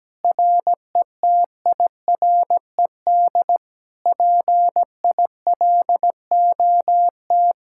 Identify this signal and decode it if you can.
Morse code, heard as a keyed tone, about 17 wpm, 700 Hz. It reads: RETIRED PILOT
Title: Speed-Racing Common QSO Phrases Encore - 17wpm